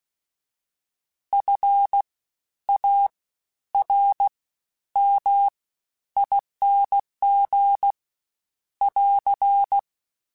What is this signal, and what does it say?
Morse code, heard as a keyed tone, about 16 wpm, 800 Hz. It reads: F A R M ING <AR>
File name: Special INGs-mp3s / Special INGs-mp3s / SlowToFaster ing @15